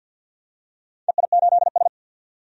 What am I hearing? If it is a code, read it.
Morse code, heard as a keyed tone, about 50 wpm, 700 Hz. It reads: EI8S
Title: DXCC-100 Call Signs - 4 Characters - 50wpm